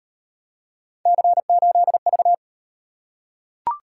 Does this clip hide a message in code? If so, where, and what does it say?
Yes: Morse code from 1.1 to 2.3 s, heard as a keyed tone, about 38 wpm, 700 Hz. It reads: C8V